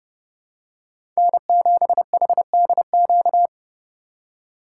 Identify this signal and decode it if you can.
Morse code, heard as a keyed tone, about 32 wpm, 700 Hz. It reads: N7HDQ
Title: USA Call Signs - 5 Characters - 4x Character Spacing - 30wpm